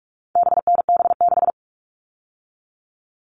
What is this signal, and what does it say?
Morse code, heard as a keyed tone, about 44 wpm, 700 Hz. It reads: BNB6
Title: All Cut Numbers - 4-Digit 45wpm